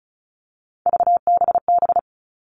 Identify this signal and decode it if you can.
Morse code, heard as a keyed tone, about 36 wpm, 700 Hz. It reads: VBB
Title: Three Challenging Letter-Number Combinations 35wpm